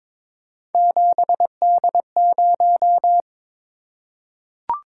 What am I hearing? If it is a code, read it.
Morse code, heard as a keyed tone, about 22 wpm, 700 Hz. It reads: 7D0
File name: practice / 22wpm / Three Letter-Number Combinations - 22wpm